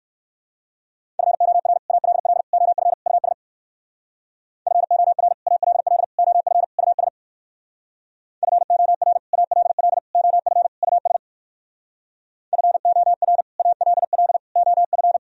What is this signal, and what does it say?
Morse code, heard as a keyed tone, about 58 wpm, 700 Hz. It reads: FOR ALL OF US FOR ALL OF US FOR ALL OF US FOR ALL OF